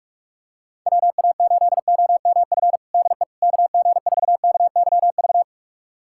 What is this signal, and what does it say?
Morse code, heard as a keyed tone, about 41 wpm, 700 Hz. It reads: WA8OMF DE KG4KYV
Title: USA Callsign DE Callsign - 6 Characters - 45wpm